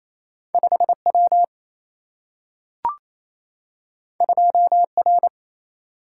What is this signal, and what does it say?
Morse code, heard as a keyed tone, about 28 wpm, 700 Hz. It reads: 5W 2R
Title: Single-Letter Single-Number 28wpm